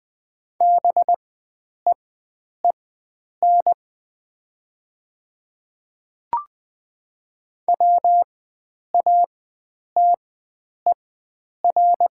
Morse code, heard as a keyed tone, about 20 wpm, spaced as for 9 wpm, 700 Hz. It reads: BEEN WATER
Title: 4x Character Spacing - Top 500 Words - 20wpm